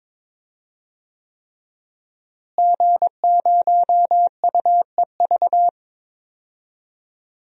Morse code, heard as a keyed tone, about 22 wpm, 700 Hz. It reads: G0UEV
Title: DXCC-100 Call Signs - 5 Characters - Repeated 3x - 22wpm